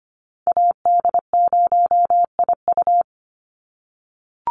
Morse code, heard as a keyed tone, about 25 wpm, 700 Hz. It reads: AD0IU
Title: Call Signs - 25wpm